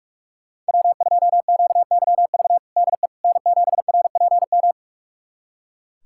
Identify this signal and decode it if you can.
Morse code, heard as a keyed tone, about 42 wpm, 700 Hz. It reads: W2QYV DE N7RPM
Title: USA Callsign DE Callsign - 5 Characters - 45wpm